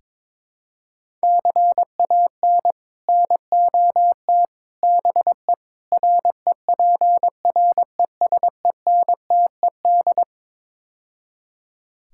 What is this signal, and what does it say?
Morse code, heard as a keyed tone, about 23 wpm, 700 Hz. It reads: CAN NOT BE REPRESENTED
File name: practice / 22wpm / Build A Phrase - 5-Word Phrases - 22wpm